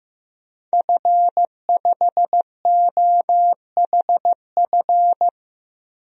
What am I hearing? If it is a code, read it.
Morse code, heard as a keyed tone, about 15 wpm, 700 Hz. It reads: F5OHF
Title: DXCC-100 Call Signs - 5 Characters - 15wpm